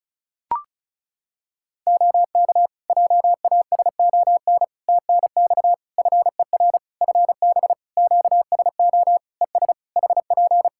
Morse code, heard as a keyed tone, about 36 wpm, 700 Hz. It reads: OK JASON TNX FER FB QSO ES HP
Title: Practice QSO - Exchange 4 - Ending - 35wpm